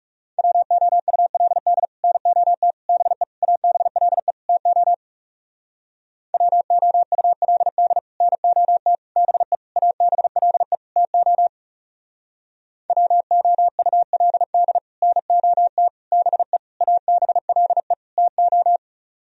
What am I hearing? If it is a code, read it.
Morse code, heard as a keyed tone, about 38 wpm, 700 Hz. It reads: WOULD NOT BE ABLE TO WOULD NOT BE ABLE TO WOULD NOT BE ABLE TO